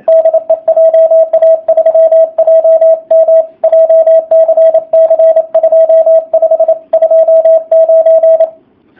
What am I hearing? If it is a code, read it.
Morse code, heard as a keyed tone, about 27 wpm, 660 Hz, with radio noise. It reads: DE JA3JM JCC 2529
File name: jh8cbh / 2019cw / sp35- w